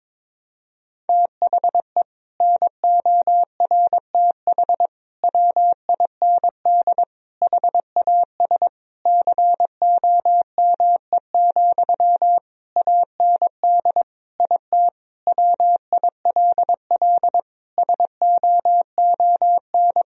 Morse code, heard as a keyed tone, about 22 wpm, 700 Hz. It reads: THE NORTH WIND HAS COME, AND IT WILL SOON
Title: Sentences from Top 1000 Words - Review Part 3 - 22wpm